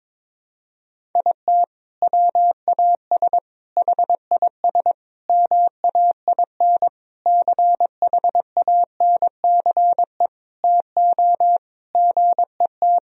Morse code, heard as a keyed tone, about 22 wpm, 700 Hz. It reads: IT WAS HIS MAIN CHANCE TO GET
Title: Sentences from Top 1000 Words - Review Part 2 - 22wpm